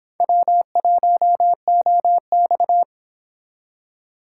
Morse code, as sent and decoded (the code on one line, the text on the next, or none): .-- .---- --- -..-
W1OX